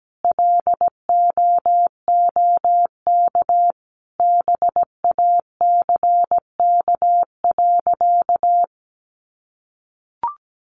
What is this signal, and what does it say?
Morse code, heard as a keyed tone, about 17 wpm, 700 Hz. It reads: LOOK BACK.
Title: Sentences from Top 300 Words 17wpm